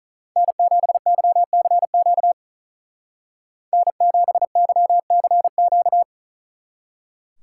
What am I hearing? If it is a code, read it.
Morse code, heard as a keyed tone, about 43 wpm, 700 Hz. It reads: N7YCQ N7YCQ
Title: Speed Racing - USA 1x3 Call Signs 35wpm